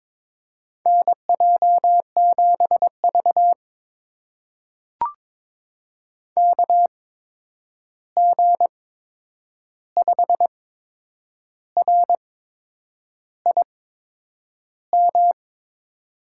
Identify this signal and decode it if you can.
Morse code, heard as a keyed tone, about 22 wpm, spaced as for 8 wpm, 700 Hz. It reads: NJ7V KG5RIM
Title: USA Call Signs - 8x Character Spacing - 22wpm